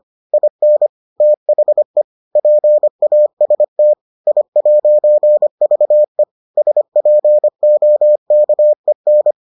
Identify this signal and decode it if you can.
Morse code, heard as a keyed tone, about 25 wpm, 600 Hz. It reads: IN THE PAST I'VE SPOKEN